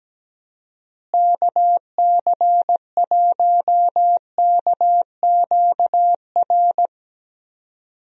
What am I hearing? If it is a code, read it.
Morse code, heard as a keyed tone, about 17 wpm, 700 Hz. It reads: KC1KQR
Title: USA Call Signs - 6 Characters - Repeated 2x - 17wpm